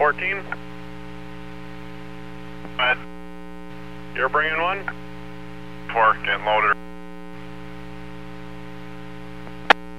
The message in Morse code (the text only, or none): none